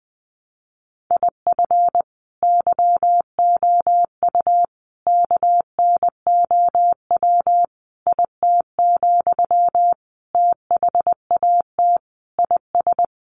Morse code, heard as a keyed tone, about 20 wpm, 700 Hz. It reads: IF YOU KNOW IT, THAT IS
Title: Sentences from Top 400 Words 20wpm